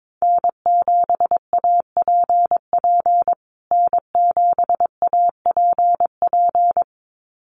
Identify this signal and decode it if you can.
Morse code, heard as a keyed tone, about 22 wpm, 700 Hz. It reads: N7APP N7APP
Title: USA Call Signs Repeated 2x 22wpm